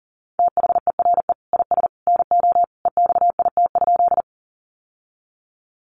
Morse code, heard as a keyed tone, about 40 wpm, 700 Hz. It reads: THERE IS NO EXIT?
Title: Sentences from Top 1100 Words 40wpm